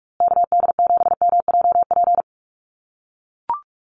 Morse code, heard as a keyed tone, about 44 wpm, 700 Hz. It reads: KD7MJP
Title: USA Call Signs Repeated 3x 45wpm